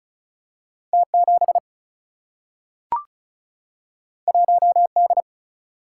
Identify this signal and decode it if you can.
Morse code, heard as a keyed tone, about 36 wpm, 700 Hz. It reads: T 7 1 D
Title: Single-Letter Single-Number 35wpm